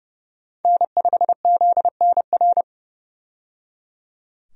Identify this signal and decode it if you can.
Morse code, heard as a keyed tone, about 31 wpm, 700 Hz. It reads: N5ZNR
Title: USA Call Signs - 5 Characters - 30wpm